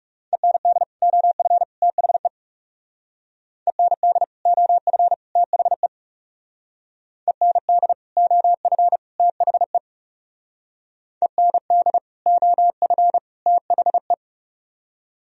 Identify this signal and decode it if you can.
Morse code, heard as a keyed tone, about 35 wpm, 700 Hz. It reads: END OF THE END OF THE END OF THE END OF THE
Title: Speed-Racing - Sets of 3 Words 30wpm